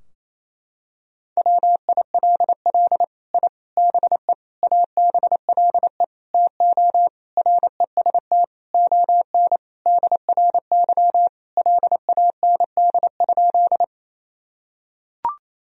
Morse code, heard as a keyed tone, about 28 wpm, 700 Hz. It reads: WILL I BE ABLE TO REST ON DRY LAND?
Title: Sentences from Top 1000 Words - Review Part 3 - 28wpm